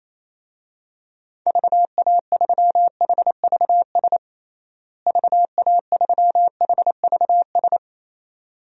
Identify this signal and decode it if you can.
Morse code, heard as a keyed tone, about 28 wpm, 700 Hz. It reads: VA3HVS VA3HVS
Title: DXCC-100 Call Signs - 6 Characters - Repeated 2x - 28wpm